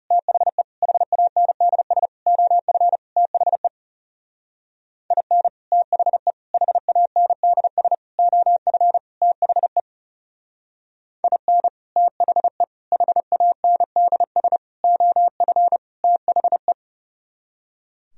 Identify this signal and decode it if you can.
Morse code, heard as a keyed tone, about 36 wpm, 700 Hz. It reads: THE HANDS OF THE IN THE HANDS OF THE IN THE HANDS OF THE
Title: Speed-Racing - Sets of 5 Words 30wpm